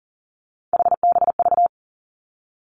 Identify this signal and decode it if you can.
Morse code, heard as a keyed tone, about 38 wpm, 700 Hz. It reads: HBV
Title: Three Challenging Letter-Number Combinations 40wpm